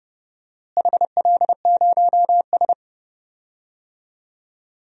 Morse code, heard as a keyed tone, about 30 wpm, 700 Hz. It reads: HL0S
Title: DXCC-100 Call Signs - 4 Characters - Repeated 3x - 30wpm